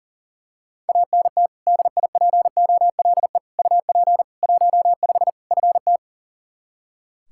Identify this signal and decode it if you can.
Morse code, heard as a keyed tone, about 39 wpm, 700 Hz. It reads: ANT DIPOLE UP 15 FT
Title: Practice QSO - Exchange 2 - Info - 40wpm